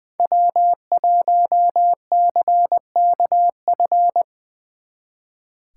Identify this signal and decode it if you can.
Morse code, heard as a keyed tone, about 23 wpm, 700 Hz. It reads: W1CKF
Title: Speed Racing - USA 1x3 Call Signs 20wpm